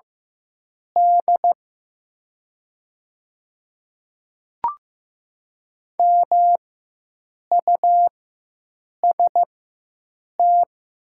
Morse code, heard as a keyed tone, about 15 wpm, spaced as for 7 wpm, 700 Hz. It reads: D MUST
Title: 4x Character Spacing - Top 500 Words - 15wpm